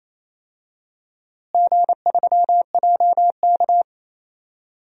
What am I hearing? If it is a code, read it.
Morse code, heard as a keyed tone, about 28 wpm, 700 Hz. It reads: G3JK
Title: DXCC-100 Call Signs - 4 Characters - Repeated 2x - 28wpm